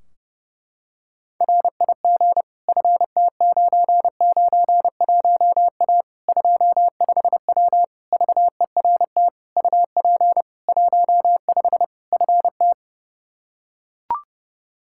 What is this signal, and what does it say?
Morse code, heard as a keyed tone, about 30 wpm, 700 Hz. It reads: RIG FT991A 25W VERT UP 15 FT
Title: Practice QSO - Exchange 2 - Info - 30wpm